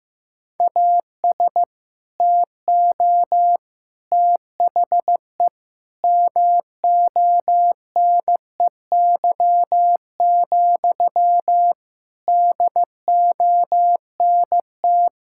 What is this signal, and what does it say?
Morse code, heard as a keyed tone, about 15 wpm, 700 Hz. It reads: AS TO THE MONEY, DONT
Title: Sentences from Top 1000 Words - Review Part 5 - 15wpm